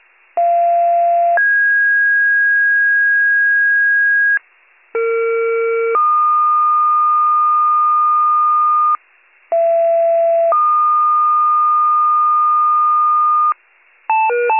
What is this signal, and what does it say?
Two-tone page, heard as a one-second tone, then a three-second tone, about 690 then 1650 Hz, about 460 then 1220 Hz, about 670 then 1220 Hz. It is heard from 0.4 to 13.5 s.